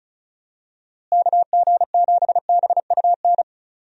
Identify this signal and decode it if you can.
Morse code, heard as a keyed tone, about 34 wpm, 700 Hz. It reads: KG7BUN
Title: USA Call Signs - 6 Characters - Repeated 2x - 35wpm